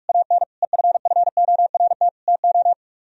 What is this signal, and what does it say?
Morse code, heard as a keyed tone, about 58 wpm, spaced as for 51 wpm, 700 Hz. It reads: AN EFFORT TO
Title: Speed-Racing - Sets of 4 Words 45wpm